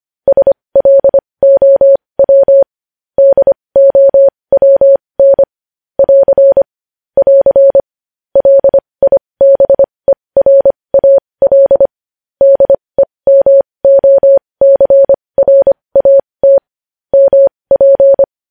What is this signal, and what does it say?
Morse code, heard as a keyed tone, about 25 wpm, spaced as for 20 wpm, 550 Hz. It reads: SLOW DOWN <AR> <AR> LIBERAL DEMOCRAT MP
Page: News Headlines in Morse Code at 20 WPM